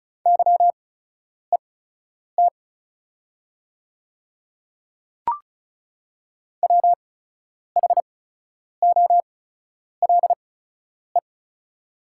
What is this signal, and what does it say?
Morse code, heard as a keyed tone, about 34 wpm, spaced as for 10 wpm, 690 Hz. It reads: YET WHOLE
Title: 8x Character Spacing - Top 500 Words - 35wpm